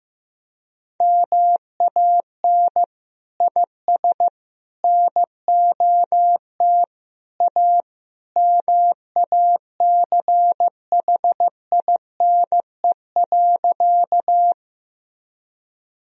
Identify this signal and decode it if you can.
Morse code, heard as a keyed tone, about 15 wpm, 700 Hz. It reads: MAN IS NOT A MACHINE.
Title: Sentences from Top 1000 Words - Review Part 2 - 15wpm